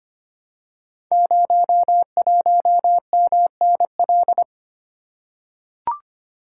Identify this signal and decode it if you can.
Morse code, heard as a keyed tone, about 25 wpm, 700 Hz. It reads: 01MNL
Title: Code Groups - 5 Letter:Numbers - 25wpm